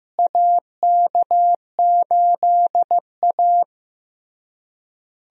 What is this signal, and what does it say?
Morse code, heard as a keyed tone, about 15 wpm, 700 Hz. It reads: AK8A
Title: USA Call Signs - 4 Characters - Repeated 3x - 15wpm